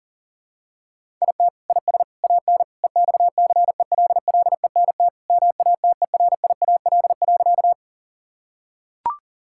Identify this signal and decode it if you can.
Morse code, heard as a keyed tone, about 40 wpm, 700 Hz. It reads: IT IS AN EXCELLENT MATERIAL.
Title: Sentences from Top 1000 Words - Review Part 5 - 40wpm